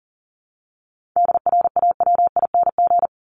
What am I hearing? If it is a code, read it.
Morse code, heard as a keyed tone, about 40 wpm, 700 Hz. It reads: DRAWING
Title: ING Suffix - Top 100 Words with Base Words 40wpm